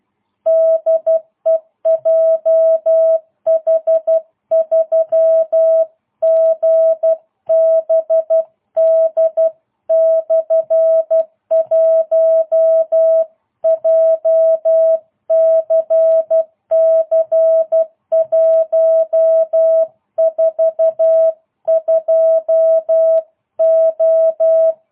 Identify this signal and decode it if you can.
Morse code, heard as a keyed tone, about 12 wpm, 650 Hz. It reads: DE JH3GBD/1 JCC 142O